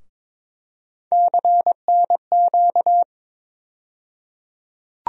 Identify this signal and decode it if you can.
Morse code, heard as a keyed tone, about 22 wpm, 700 Hz. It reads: CNQ
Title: USA Call Sign Postfixes - 3 Letters 22wpm